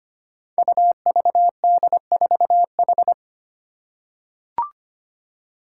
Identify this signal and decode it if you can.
Morse code, heard as a keyed tone, about 25 wpm, 700 Hz. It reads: UVD4H
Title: Five Challenging Letter-Number Combinations 25wpm